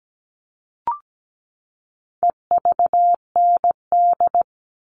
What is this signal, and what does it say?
Morse code, heard as a keyed tone, about 17 wpm, 700 Hz. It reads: EVND
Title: All Cut Numbers - 4-Digit 17wpm